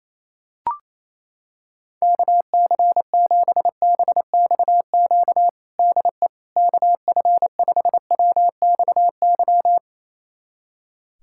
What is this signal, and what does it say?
Morse code, heard as a keyed tone, about 28 wpm, 700 Hz. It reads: KC7BXQ DE KF5WXY
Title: USA Callsign DE Callsign 28wpm